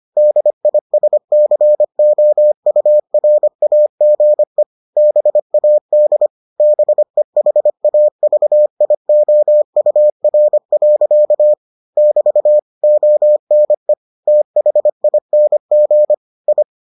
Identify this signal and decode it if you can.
Morse code, heard as a keyed tone, about 25 wpm, 600 Hz. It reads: DISCOURAGE BAD BEHAVIOUR. <BT> ONE THING I